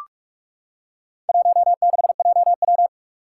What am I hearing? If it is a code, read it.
Morse code, heard as a keyed tone, about 41 wpm, 700 Hz. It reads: 16JW
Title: Four Challenging Letter-Number Combinations 45wpm